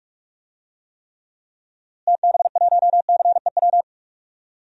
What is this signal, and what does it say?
Morse code, heard as a keyed tone, about 46 wpm, 700 Hz. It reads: TB1CEW
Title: DXCC-100 Call Signs - 6 Characters - Repeated 2x - 45wpm